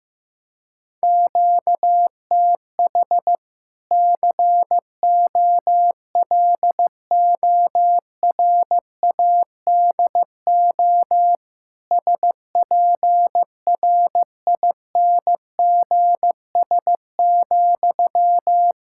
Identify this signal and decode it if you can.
Morse code, heard as a keyed tone, about 16 wpm, 700 Hz. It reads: QTH COLORADO SPRINGS,
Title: Practice QSO - Exchange 1 - Beginning - 15wpm